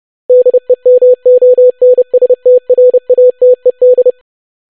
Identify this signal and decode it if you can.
Morse code, heard as a keyed tone, about 30 wpm, 500 Hz. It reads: DEMONSTRATED
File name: longwords-25wpm